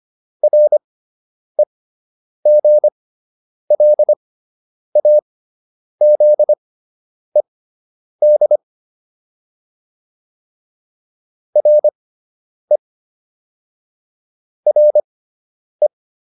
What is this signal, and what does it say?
Morse code, heard as a keyed tone, about 25 wpm, spaced as for 9 wpm, 600 Hz. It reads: REGLAZED RE RE